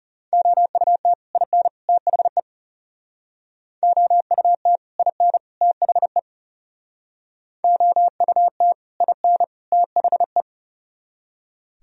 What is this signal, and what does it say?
Morse code, heard as a keyed tone, about 37 wpm, 700 Hz. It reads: OUT IN THE OUT IN THE OUT IN THE
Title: Speed-Racing - Sets of 3 Words 30wpm